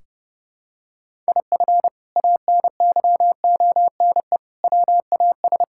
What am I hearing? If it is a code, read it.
Morse code, heard as a keyed tone, about 32 wpm, 700 Hz. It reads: IF ANYONE WAS
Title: Sentences from Top 1600 Words 30wpm